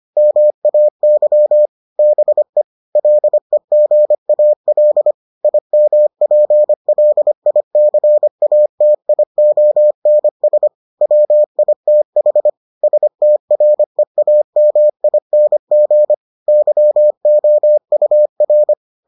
Morse code, heard as a keyed tone, about 25 wpm, 600 Hz. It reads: MAY BE LEGAL IMPLICATIONS WITH STREAMING YOUR